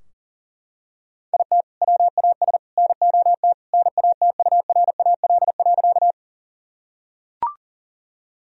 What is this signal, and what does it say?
Morse code, heard as a keyed tone, about 40 wpm, 700 Hz. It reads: IT WAS NOT NATURAL.
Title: Sentences from Top 1000 Words - Review Part 4 - 40wpm